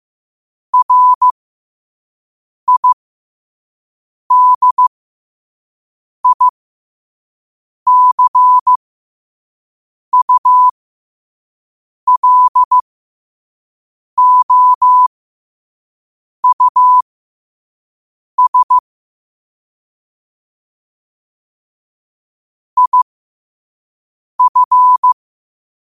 Morse code, heard as a keyed tone, about 15 wpm, spaced as for 5 wpm, 1000 Hz. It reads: RIDICULOUS IF
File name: qotdfiles / 5wpm